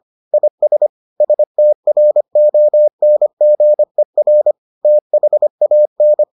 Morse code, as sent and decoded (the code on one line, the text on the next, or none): .. ... / ... - .-. --- -. --. .-. / - .... .- -.
IS STRONGER THAN